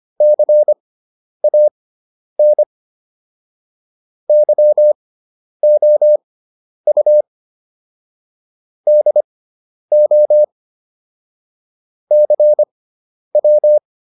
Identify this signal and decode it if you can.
Morse code, heard as a keyed tone, about 25 wpm, spaced as for 10 wpm, 600 Hz. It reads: CAN YOU DO CW